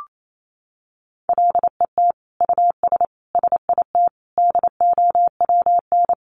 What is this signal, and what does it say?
Morse code, heard as a keyed tone, about 28 wpm, 700 Hz. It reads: LET US SIT DOWN
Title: Sentences from Top 1400 Words 28wpm